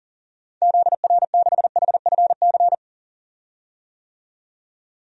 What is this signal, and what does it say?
Morse code, heard as a keyed tone, about 40 wpm, 700 Hz. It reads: ZR6HFC
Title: DXCC-100 Call Signs - 6 Characters - 40wpm